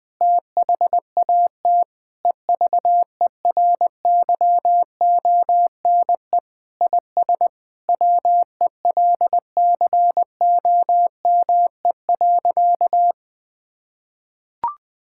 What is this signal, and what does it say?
Morse code, heard as a keyed tone, about 20 wpm, 700 Hz. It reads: THAT EVERYONE IS WELCOME.
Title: Sentences with 7-Word Phrases - 20wpm